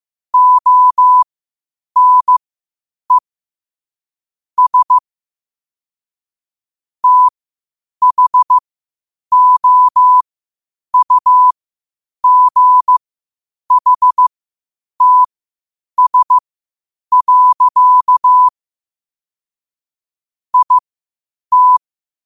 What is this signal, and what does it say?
Morse code, heard as a keyed tone, about 15 wpm, spaced as for 8 wpm, 1000 Hz. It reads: ONES THOUGHTS. IT
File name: qotdfiles / 8wpm